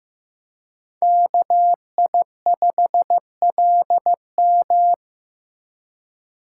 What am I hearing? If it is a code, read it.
Morse code, heard as a keyed tone, about 18 wpm, 700 Hz. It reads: KI5LM
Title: Speed Racing - USA 2x2 Call Signs 15wpm